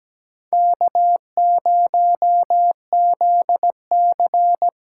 Morse code, heard as a keyed tone, about 17 wpm, 700 Hz. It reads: K0ZC
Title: USA Call Signs - 4 Characters - 2x Character Spacing - 17wpm